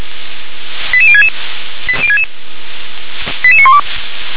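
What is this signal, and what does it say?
Five-tone selective call (ZVEI), heard as quick runs of five tones, tones of about 70 ms. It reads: 88066 88912
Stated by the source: Received on an SDR, ZVEI five-tone selective call, code 88175.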